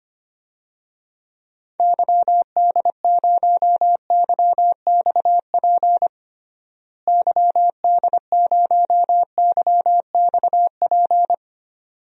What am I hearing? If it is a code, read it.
Morse code, heard as a keyed tone, about 25 wpm, 700 Hz. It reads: YD0YXP YD0YXP
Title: DXCC-100 Call Signs - 6 Characters - Repeated 2x - 25wpm